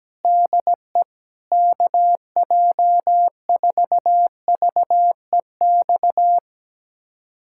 Morse code, heard as a keyed tone, about 17 wpm, 700 Hz. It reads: DE KJ4VEX
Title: USA Callsign DE Callsign - 6 Characters - 17wpm